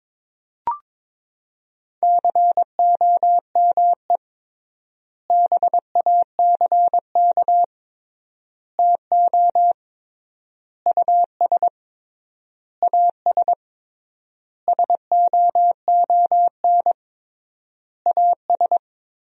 Morse code, heard as a keyed tone, about 22 wpm, 700 Hz. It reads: COME BACK TO US AS SOON AS
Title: 3x Word-Spacing - Sentences from Top 500 Words - 22wpm